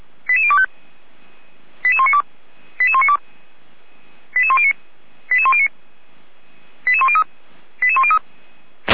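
Five-tone selective call (ZVEI), heard as quick runs of five tones, tones of about 80 ms. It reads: 88025 88192 88192 88109 88109 88103 88103
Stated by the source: ZVEI five-tone selective call 88102 received on SDR, from the feuerwehr-orlamuende.org fire brigade audio archive.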